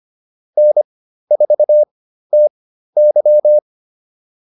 Morse code, heard as a keyed tone, about 25 wpm, spaced as for 13 wpm, 600 Hz. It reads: N4TY
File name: QSO109 13